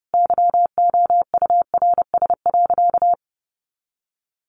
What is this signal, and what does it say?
Morse code, heard as a keyed tone, about 30 wpm, 700 Hz. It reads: YOURS.
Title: A Sound Of Thunder 30wpm